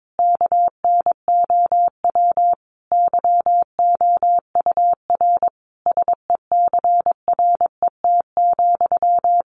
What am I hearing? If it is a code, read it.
Morse code, heard as a keyed tone, about 22 wpm, 700 Hz. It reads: KNOW YOUR SECRET,
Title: Sentences from Top 1200 Words 22wpm